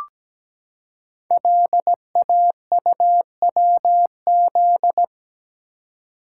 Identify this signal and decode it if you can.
Morse code, heard as a keyed tone, about 17 wpm, 700 Hz. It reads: LAUWZ
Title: Code Groups - 5 Letter:Numbers - 17wpm